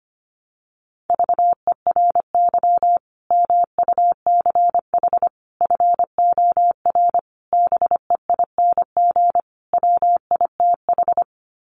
Morse code, heard as a keyed tone, about 25 wpm, 700 Hz. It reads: VERY MUCH FOR BEING WITH
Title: Sets of 5 Words 25wpm